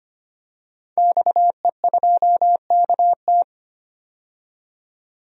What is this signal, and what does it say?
Morse code, heard as a keyed tone, about 25 wpm, 700 Hz. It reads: XE2KT
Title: DXCC-100 Call Signs - 5 Characters - 25wpm